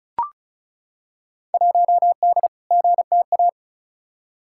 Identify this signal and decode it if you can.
Morse code, heard as a keyed tone, about 36 wpm, 700 Hz. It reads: 1D GTA
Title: Field Day Exchanges - 35wpm